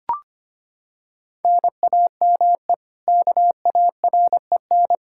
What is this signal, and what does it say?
Morse code, heard as a keyed tone, about 25 wpm, 700 Hz. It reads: NAME KAREN
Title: Practice QSO - Exchange 1 - Beginning - 25wpm